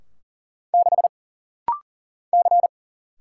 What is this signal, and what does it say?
Morse code, heard as a keyed tone, about 42 wpm, 700 Hz. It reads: 6 C
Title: Single Letter-Number - Rapid-Fire 40wpm